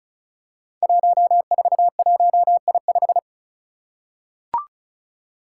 Morse code, heard as a keyed tone, about 36 wpm, 700 Hz. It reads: 141I5
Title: Five Challenging Letter-Number Combinations 35wpm